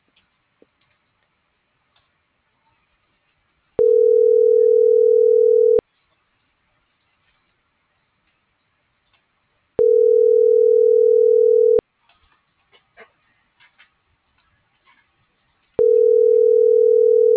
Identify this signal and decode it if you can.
Ringback tone (North America), 440 + 480 Hz.